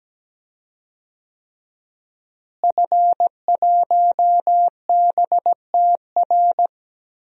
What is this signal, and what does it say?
Morse code, heard as a keyed tone, about 17 wpm, 700 Hz. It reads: F1BTR
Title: DXCC-100 Call Signs - 5 Characters - Repeated 3x - 17wpm